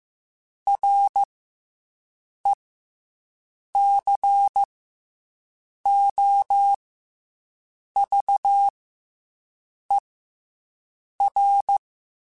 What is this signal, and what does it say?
Morse code, heard as a keyed tone, about 15 wpm, 800 Hz. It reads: R E C O V E R
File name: Special INGs / Special INGs-mp3s / SlowToFaster ing @13